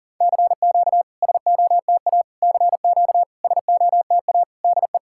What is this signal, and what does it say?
Morse code, heard as a keyed tone, about 40 wpm, 700 Hz. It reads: CQ SOTA CQ SOTA DE
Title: SOTA QSO Practice 40wpm